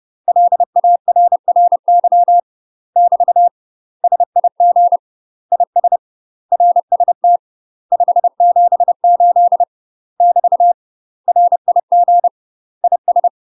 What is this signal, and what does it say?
Morse code, heard as a keyed tone, about 30 wpm, 700 Hz. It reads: LARRY <BT> SIG IS RST 578 <BT> RIG IS
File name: UK MASTER QSOs TRAINING / 30WPM / UK MASTER QSOs 0034